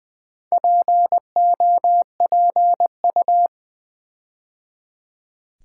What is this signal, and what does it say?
Morse code, heard as a keyed tone, about 20 wpm, 700 Hz. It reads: POPU
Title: Build A Word - 4 Letters At A Time - 10-Letter Words - 20wpm